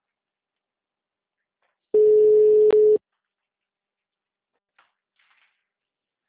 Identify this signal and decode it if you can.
Ringback tone (Europe, 425 Hz).